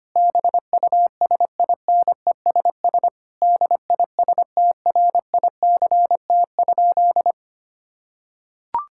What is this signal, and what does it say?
Morse code, heard as a keyed tone, about 25 wpm, 700 Hz. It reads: BUSINESS DISTRICT?
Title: Sentences from Top 1600 Words 25wpm